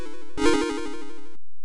There is no code.